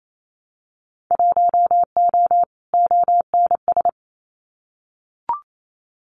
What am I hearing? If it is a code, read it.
Morse code, heard as a keyed tone, about 28 wpm, 700 Hz. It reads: 1O ONS
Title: Winter Field Day Exchanges Encore 28wpm